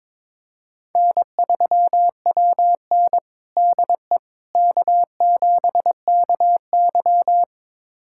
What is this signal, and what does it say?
Morse code, heard as a keyed tone, about 22 wpm, 700 Hz. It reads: N3WN DE K7KY
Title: USA Callsign DE Callsign - 4 Characters - 22wpm